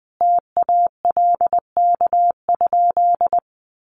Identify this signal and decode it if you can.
Morse code, heard as a keyed tone, about 20 wpm, 700 Hz. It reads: TALK?
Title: Sentences from Top 1100 Words 20wpm